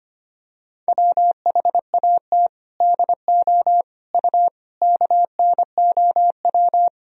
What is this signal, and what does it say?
Morse code, heard as a keyed tone, about 25 wpm, 700 Hz. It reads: WHAT DO U KNOW
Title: Sentences from Top 100 Words and Abbreviations 25wpm